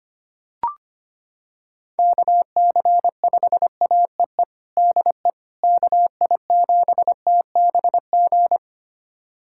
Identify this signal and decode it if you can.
Morse code, heard as a keyed tone, about 25 wpm, 700 Hz. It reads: KC5AEE DE KI7TBG
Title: USA Callsign DE Callsign - 6 Characters - 25wpm